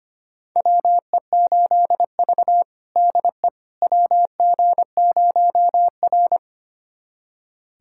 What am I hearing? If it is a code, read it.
Morse code, heard as a keyed tone, about 25 wpm, 700 Hz. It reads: WE8V DE WG0R
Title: USA Callsign DE Callsign - 4 Characters - 25wpm